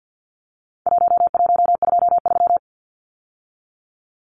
Morse code, heard as a keyed tone, about 50 wpm, 700 Hz. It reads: 1123
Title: Numbers - 4-Digit 50wpm